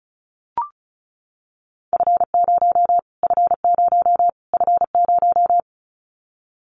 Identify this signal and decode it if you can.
Morse code, heard as a keyed tone, about 34 wpm, 700 Hz. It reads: F0 F0 F0
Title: Callsign Prefixes for Top 25 DXCC - Repeated 3x - 35wpm